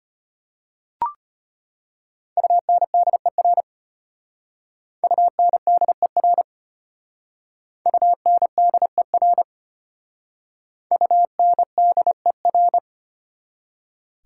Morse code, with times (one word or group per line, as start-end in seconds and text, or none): 2.4-3.6 UNDER
5.0-6.4 UNDER
7.9-9.4 UNDER
10.9-12.8 UNDER